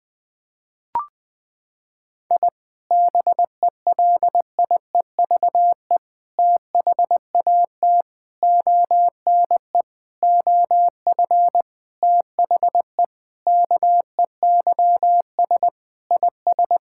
Morse code, heard as a keyed tone, about 21 wpm, 700 Hz. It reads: I BELIEVE THAT ONE OF THE KEYS IS
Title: Sentences with 4-Word Phrases - 20wpm